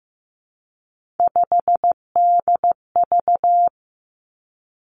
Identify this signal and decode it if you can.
Morse code, heard as a keyed tone, about 15 wpm, 700 Hz. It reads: 5DV
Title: Three Challenging Letter-Number Combinations 15wpm